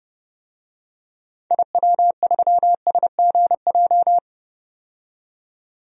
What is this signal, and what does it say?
Morse code, heard as a keyed tone, about 30 wpm, 700 Hz. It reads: IW3SGJ
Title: DXCC-100 Call Signs - 6 Characters - 30wpm